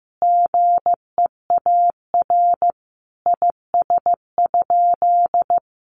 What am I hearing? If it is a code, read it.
Morse code, heard as a keyed tone, about 15 wpm, 700 Hz. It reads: GEAR IS?
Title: Sentences from Top 1400 Words 15wpm